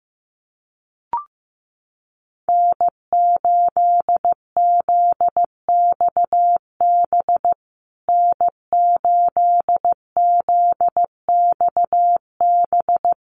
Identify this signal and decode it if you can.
Morse code, heard as a keyed tone, about 15 wpm, 700 Hz. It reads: N8ZXB N8ZXB
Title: USA Call Signs Repeated 3x 15wpm